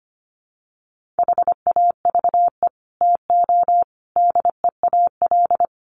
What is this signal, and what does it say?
Morse code, heard as a keyed tone, about 25 wpm, 700 Hz. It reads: HAVE TO DEAL
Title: Sets of 5 Words Encore2 25wpm